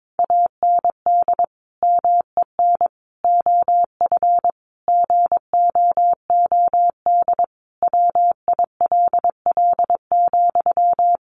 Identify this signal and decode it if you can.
Morse code, heard as a keyed tone, about 22 wpm, 700 Hz. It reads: AND MEN OF GOOD WILL,
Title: Sentences from Top 1400 Words 22wpm